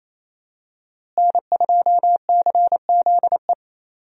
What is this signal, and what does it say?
Morse code, heard as a keyed tone, about 28 wpm, 700 Hz. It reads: N2CZE
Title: USA Call Signs - 5 Characters - Repeated 2x - 28wpm